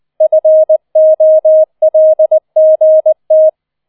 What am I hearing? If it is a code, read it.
Morse code, heard as a keyed tone, about 20 wpm, 630 Hz. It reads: FOLGT